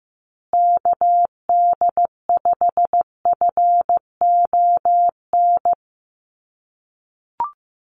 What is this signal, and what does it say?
Morse code, heard as a keyed tone, about 15 wpm, 700 Hz. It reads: KD5FON
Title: USA Call Signs Repeated 3x 15wpm